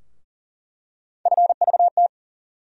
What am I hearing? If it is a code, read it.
Morse code, heard as a keyed tone, about 40 wpm, 700 Hz. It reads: FVT
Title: USA Call Sign Postfixes - 3 Letters 40wpm